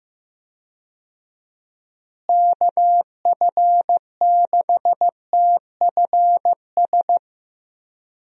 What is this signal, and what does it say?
Morse code, heard as a keyed tone, about 15 wpm, 700 Hz. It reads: KF6TFS
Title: USA Call Signs - 6 Characters - Repeated 2x - 15wpm